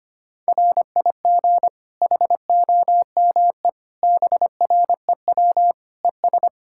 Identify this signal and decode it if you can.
Morse code, heard as a keyed tone, about 25 wpm, 700 Hz. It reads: RIG HOME BREW ES
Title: Practice QSO - Exchange 2 - Info - 25wpm